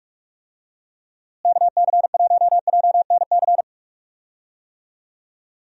Morse code, heard as a keyed tone, about 46 wpm, 700 Hz. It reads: KC1JNC
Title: USA Call Signs - 6 Characters - Repeated 2x - 45wpm